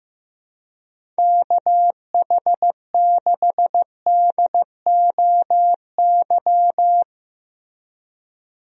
Morse code, heard as a keyed tone, about 15 wpm, 700 Hz. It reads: KH6DOY
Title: DXCC-100 Call Signs - 6 Characters - Repeated 2x - 15wpm